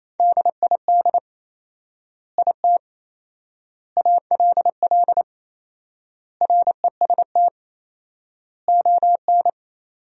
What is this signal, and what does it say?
Morse code, heard as a keyed tone, about 28 wpm, 700 Hz. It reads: DID IT ALL REST ON
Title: 4x Word-Spacing - Sentences from Top 500 Words - 28wpm